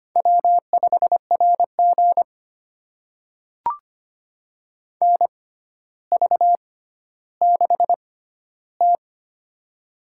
Morse code, heard as a keyed tone, about 25 wpm, spaced as for 13 wpm, 700 Hz. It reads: W5RG NV6T
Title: USA Call Signs - 4 Characters - 6x Character Spacing - 25wpm